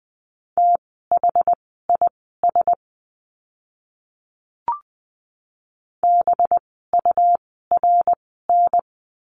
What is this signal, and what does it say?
Morse code, heard as a keyed tone, about 20 wpm, spaced as for 15 wpm, 700 Hz. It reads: THIS BURN
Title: 185 Lesson 30f - Letter b - Words 20wpm